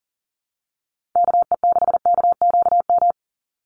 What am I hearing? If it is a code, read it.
Morse code, heard as a keyed tone, about 40 wpm, 700 Hz. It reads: KE6KQM
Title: USA Call Signs 40wpm